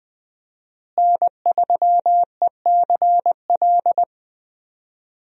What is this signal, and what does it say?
Morse code, heard as a keyed tone, about 21 wpm, 700 Hz. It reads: N3ECL
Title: USA Call Signs - 5 Characters - Repeated 3x - 20wpm